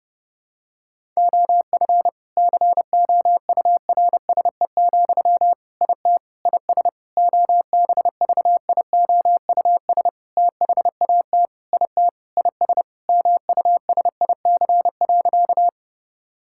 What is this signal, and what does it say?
Morse code, heard as a keyed tone, about 30 wpm, 700 Hz. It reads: OF COURSE, IT IS OBVIOUS THAT IT IS MUSIC.
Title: Sentences from Top 1600 Words 30wpm